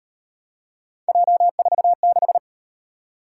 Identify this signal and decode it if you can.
Morse code, heard as a keyed tone, about 37 wpm, 700 Hz. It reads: J46